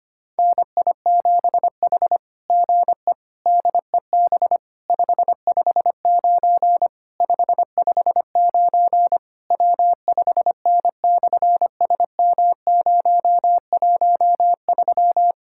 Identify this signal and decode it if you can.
Morse code, heard as a keyed tone, about 24 wpm, 700 Hz. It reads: NI7H GE DEB 559 559 W5N/SM013
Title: SOTA QSO Practice 25wpm